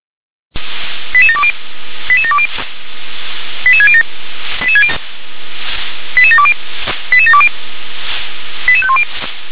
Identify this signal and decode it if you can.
Five-tone selective call (ZVEI), heard as quick runs of five tones, tones of about 70 ms. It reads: 88720 88720 88607 88720 88720 88510